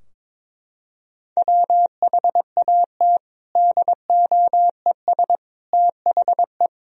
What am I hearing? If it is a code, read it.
Morse code, heard as a keyed tone, about 22 wpm, 700 Hz. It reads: WHAT DOES THE
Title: Sentences from Top 1000 Words - Review Part 2 - 22wpm